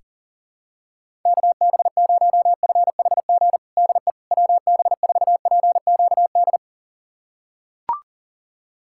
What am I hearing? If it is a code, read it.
Morse code, heard as a keyed tone, about 37 wpm, 700 Hz. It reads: KB0FHG DE WB4PQD
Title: USA Callsign DE Callsign - 6 Characters - 40wpm